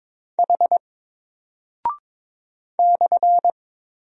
Morse code, heard as a keyed tone, about 22 wpm, 700 Hz. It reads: H /
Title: Single Character (Letter:Number:Punctuation:Prosign) Alt 22wpm